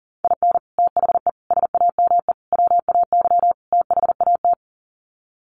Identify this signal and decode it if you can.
Morse code, heard as a keyed tone, about 40 wpm, 700 Hz. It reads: IN THE SAME WAY THAT
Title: Sets of 5 Words 40wpm